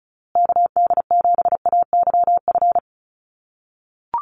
Morse code, heard as a keyed tone, about 34 wpm, 700 Hz. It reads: KD7AYF
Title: USA Call Signs Repeated 3x 35wpm